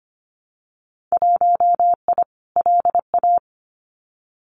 Morse code, heard as a keyed tone, about 25 wpm, 700 Hz. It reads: 1I LA
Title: Winter Field Day Exchanges Encore 25wpm